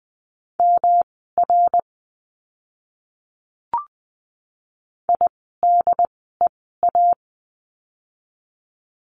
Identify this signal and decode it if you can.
Morse code, heard as a keyed tone, about 20 wpm, spaced as for 14 wpm, 700 Hz. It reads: MR IDEA
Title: 108 Lesson 18f - Letter m - Words 20wpm